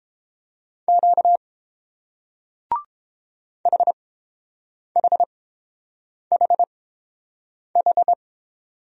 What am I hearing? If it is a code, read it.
Morse code, heard as a keyed tone, about 30 wpm, spaced as for 7 wpm, 700 Hz. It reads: Q HHHH